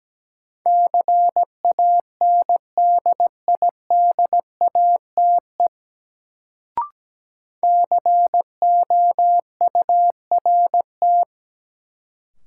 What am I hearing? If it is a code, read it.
Morse code, heard as a keyed tone, about 17 wpm, 700 Hz. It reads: CANDIDATE COURT
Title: Sentences from Top 1600 Words - New Word Drill - 17wpm